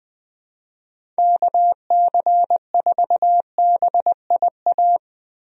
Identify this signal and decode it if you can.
Morse code, heard as a keyed tone, about 20 wpm, 700 Hz. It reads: KC4BIA
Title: USA Call Signs - 6 Characters - Repeated 2x - 20wpm